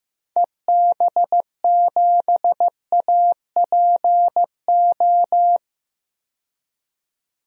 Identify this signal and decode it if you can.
Morse code, heard as a keyed tone, about 15 wpm, 700 Hz. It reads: EB7APO
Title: DXCC-100 Call Signs - 6 Characters - 15wpm